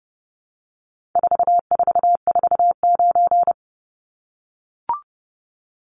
Morse code, heard as a keyed tone, about 30 wpm, 700 Hz. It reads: 4449
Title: Numbers - 4-Digit 30wpm